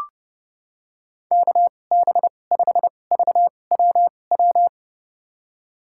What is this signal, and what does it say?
Morse code, heard as a keyed tone, about 32 wpm, spaced as for 22 wpm, 700 Hz. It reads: KB5VWW
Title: USA Call Signs - 2x Character Spacing - 30wpm